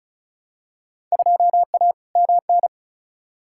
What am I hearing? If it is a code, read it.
Morse code, heard as a keyed tone, about 36 wpm, 700 Hz. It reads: 2A MN
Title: Field Day Exchanges - 35wpm